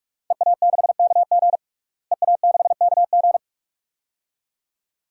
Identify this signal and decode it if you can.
Morse code, heard as a keyed tone, about 42 wpm, 700 Hz. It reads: EA6KG EA6KG
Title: DXCC-100 Call Signs - 5 Characters - Repeated 2x - 45wpm